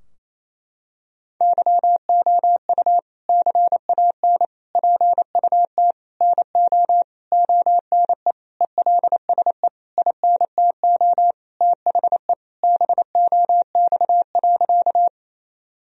Morse code, heard as a keyed tone, about 28 wpm, 700 Hz. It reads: YOU CAN PUT NO ONE ELSE INTO THE BOX.
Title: Sentences from Top 1000 Words - Review Part 2 - 28wpm